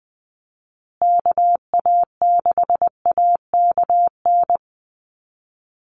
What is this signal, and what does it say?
Morse code, heard as a keyed tone, about 20 wpm, 700 Hz. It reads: KA6AKN
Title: USA Call Signs Repeated 3x 20wpm